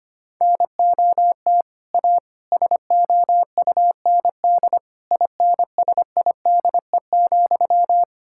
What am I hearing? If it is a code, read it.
Morse code, heard as a keyed tone, about 25 wpm, 700 Hz. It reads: NOT A SOUND INSIDE,
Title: Sentences from Top 1000 Words - Review Part 6 - 25wpm